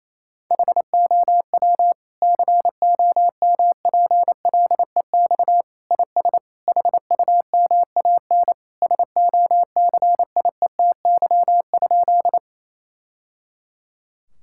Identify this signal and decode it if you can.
Morse code, heard as a keyed tone, about 28 wpm, 700 Hz. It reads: HOW COMPLEX IS HUMAN SOCIETY?
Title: Sentences from Top 1000 Words - Review Part 6 - 28wpm